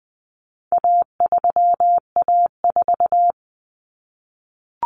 Morse code, heard as a keyed tone, about 20 wpm, 700 Hz. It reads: A3A4
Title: Common Cut Numbers - 4-Digit 20wpm